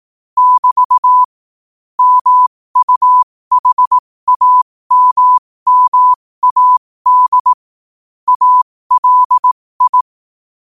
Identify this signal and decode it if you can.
Morse code, heard as a keyed tone, about 18 wpm, spaced as for 15 wpm, 1000 Hz. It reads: <BT> MUHAMMAD ALI
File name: qotdfiles / 15wpm